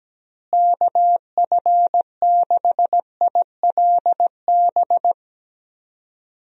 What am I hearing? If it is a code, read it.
Morse code, heard as a keyed tone, about 17 wpm, 700 Hz. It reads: KF6ILB
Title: USA Call Signs - 6 Characters - Repeated 2x - 17wpm